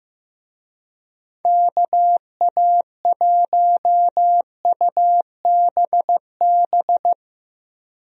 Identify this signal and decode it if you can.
Morse code, heard as a keyed tone, about 15 wpm, 700 Hz. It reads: KA1UBB
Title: USA Call Signs - 6 Characters - Repeated 2x - 15wpm